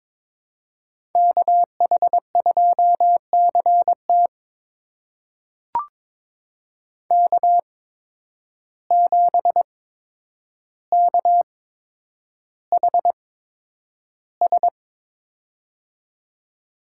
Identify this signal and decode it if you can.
Morse code, heard as a keyed tone, about 22 wpm, spaced as for 9 wpm, 700 Hz. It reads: KH2CT K7KHS
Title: USA Call Signs - 5 Characters - 8x Character Spacing - 22wpm